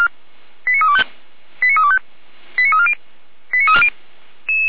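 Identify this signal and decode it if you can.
Five-tone selective call (ZVEI), heard as quick runs of five tones, tones of about 70 ms. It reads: 89325 89325 89340 89340
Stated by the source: ZVEI five-tone selective call, code 89509, received by SDR, feuerwehr-orlamuende fire brigade audio server.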